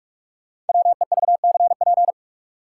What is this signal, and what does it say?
Morse code, heard as a keyed tone, about 42 wpm, 700 Hz. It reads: WEVCP
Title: Code Groups - 5 Letter:Numbers - 45wpm